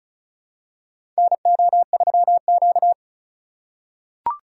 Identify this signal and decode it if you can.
Morse code, heard as a keyed tone, about 34 wpm, 700 Hz. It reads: NO3Q
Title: USA Call Signs - 4 Characters - 4x Character Spacing - 35wpm